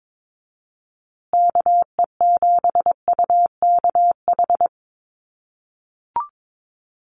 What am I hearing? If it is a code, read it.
Morse code, heard as a keyed tone, about 22 wpm, 700 Hz. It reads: KE7UKH
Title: USA Call Signs Repeated 3x 22wpm